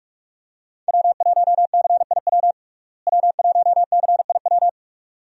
Morse code, heard as a keyed tone, about 46 wpm, 700 Hz. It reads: W1CIW W1CIW
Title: USA Call Signs - 5 Characters - Repeated 2x - 45wpm